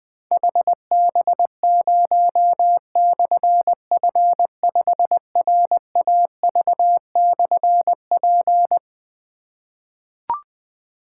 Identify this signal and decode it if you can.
Morse code, heard as a keyed tone, about 21 wpm, 700 Hz. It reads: HB0/F5RAV/P
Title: Difficult Call Signs 20wpm